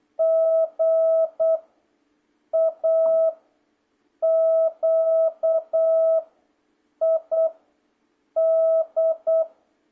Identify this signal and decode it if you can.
Morse code, heard as a keyed tone, about 8 wpm, 660 Hz. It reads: G A Q I D